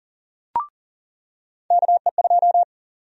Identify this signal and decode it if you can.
Morse code, heard as a keyed tone, about 40 wpm, 700 Hz. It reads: KE2
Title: USA Call Sign Prefixes - 40wpm